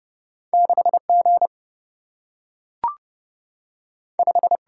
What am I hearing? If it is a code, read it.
Morse code, heard as a keyed tone, about 30 wpm, 700 Hz. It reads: 6 G 5
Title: Single-Letter Single-Number 30wpm